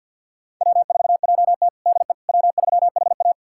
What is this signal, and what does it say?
Morse code, heard as a keyed tone, about 51 wpm, 700 Hz. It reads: W4JT DE W3HA
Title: USA Callsign DE Callsign - 4 Characters - 50wpm